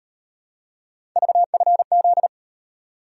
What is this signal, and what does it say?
Morse code, heard as a keyed tone, about 38 wpm, 700 Hz. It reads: VFZ